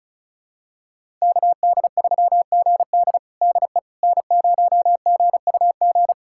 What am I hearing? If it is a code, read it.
Morse code, heard as a keyed tone, about 36 wpm, 690 Hz. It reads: KD3GD DE N0GUG
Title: USA Callsign DE Callsign - 5 Characters - 35wpm